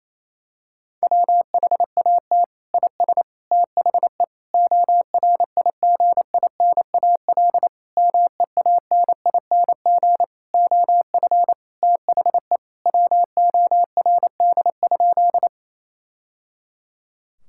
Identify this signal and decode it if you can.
Morse code, heard as a keyed tone, about 28 wpm, 700 Hz. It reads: WHAT IS THE ORIGINAL MEANING OF THE WORD?
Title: Sentences from Top 1000 Words - Review Part 6 - 28wpm